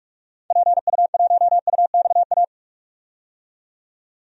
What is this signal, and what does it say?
Morse code, heard as a keyed tone, about 46 wpm, 700 Hz. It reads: PU1UXA
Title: DXCC-100 Call Signs - 6 Characters - 45wpm